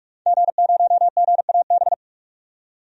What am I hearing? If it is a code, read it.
Morse code, heard as a keyed tone, about 46 wpm, 700 Hz. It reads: G0GAB
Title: DXCC-100 Call Signs - 5 Characters - 45wpm